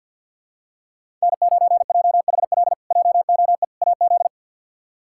Morse code, heard as a keyed tone, about 50 wpm, 700 Hz. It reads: N9JHL JOE AZ
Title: K1USN SST QSO Practice - 50wpm